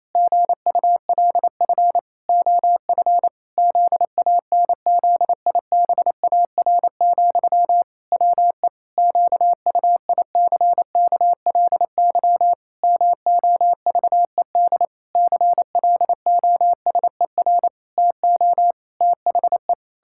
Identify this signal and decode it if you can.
Morse code, heard as a keyed tone, about 28 wpm, 700 Hz. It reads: GULF OF ZANZIBAR, WE QUICKLY MOVED CLOSER TO THE
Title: Pangram Challenge v2 - 28wpm